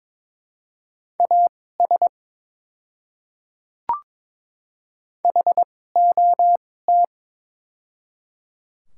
Morse code, heard as a keyed tone, about 22 wpm, spaced as for 16 wpm, 700 Hz. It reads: AS HOT
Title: 2x Character Spacing - Top 200 Words - 22wpm